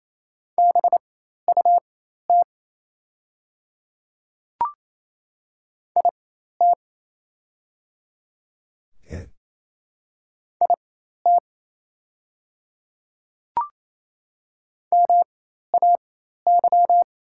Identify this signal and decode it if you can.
Morse code, heard as a keyed tone, about 28 wpm, spaced as for 13 wpm, 690 Hz. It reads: BUT IT IT MAY
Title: 4x Character Spacing - Top 100 Words - 28wpm